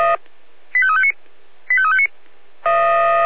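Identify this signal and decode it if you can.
Five-tone selective call (ZVEI), heard as quick runs of five tones, tones of about 70 ms. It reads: 85359 85359
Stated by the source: ZVEI five-tone call 85352 received by SDR.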